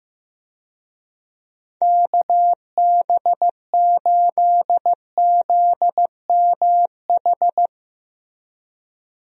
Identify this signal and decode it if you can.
Morse code, heard as a keyed tone, about 15 wpm, 700 Hz. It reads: KB8ZMH
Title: USA Call Signs - 6 Characters - Repeated 3x - 15wpm